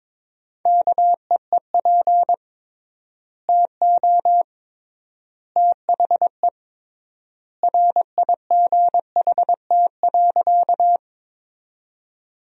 Morse code, heard as a keyed tone, about 21 wpm, 700 Hz. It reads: KEEP TO THE RIGHT.
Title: 3x Word-Spacing - Sentences from Top 300 Words - 22wpm